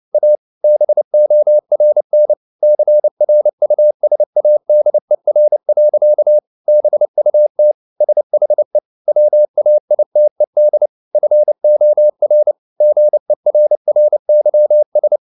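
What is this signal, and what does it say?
Morse code, heard as a keyed tone, about 29 wpm, 600 Hz. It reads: A BORN CRUSADER. BUT SHE WAITED FOR GERRYS
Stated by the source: Morse code at 29 wpm, sending Ken Follett's novel Never in English, from CW med Gnister.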